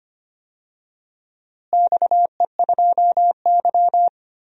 Morse code, heard as a keyed tone, about 25 wpm, 700 Hz. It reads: XE2Y
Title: DXCC-100 Call Signs - 4 Characters - Repeated 2x - 25wpm